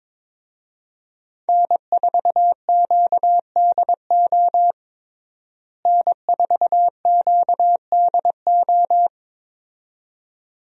Morse code, heard as a keyed tone, about 22 wpm, 700 Hz. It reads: N4QDO N4QDO
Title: USA Call Signs - 5 Characters - Repeated 2x - 22wpm